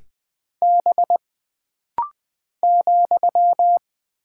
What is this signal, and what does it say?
Morse code, heard as a keyed tone, about 20 wpm, 700 Hz. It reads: B ,
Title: Single Character (Letter:Number:Punctuation:Prosign) - Rapid-Fire - 20wpm